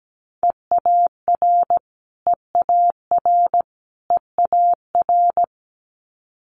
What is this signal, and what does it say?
Morse code, heard as a keyed tone, about 17 wpm, 700 Hz. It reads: EAR EAR EAR
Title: Multi-letter Phonemes Repeated 3x 17wpm